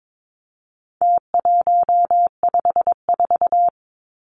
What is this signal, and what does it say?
Morse code, heard as a keyed tone, about 22 wpm, 700 Hz. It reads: T154
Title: Numbers - Contest Style - 4-Digit 22wpm